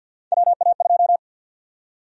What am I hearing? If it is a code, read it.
Morse code, heard as a keyed tone, about 48 wpm, 700 Hz. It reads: WA2
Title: USA Call Sign Prefixes - 50wpm